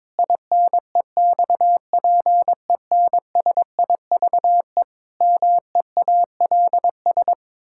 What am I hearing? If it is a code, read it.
Morse code, heard as a keyed tone, about 22 wpm, 700 Hz. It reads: INEXPENSIVE MEALS
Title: Pangram Challenge v2 - 22wpm